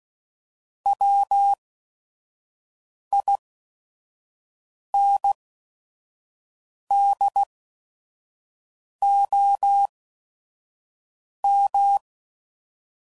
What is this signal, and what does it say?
Morse code, heard as a keyed tone, about 16 wpm, spaced as for 5 wpm, 800 Hz. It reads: WINDOM
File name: CW Code Groups @5WPM / fromSM's / Antennas